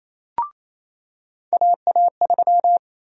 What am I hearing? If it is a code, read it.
Morse code, heard as a keyed tone, about 27 wpm, 700 Hz. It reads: AA3
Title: USA Call Sign Prefixes - 28wpm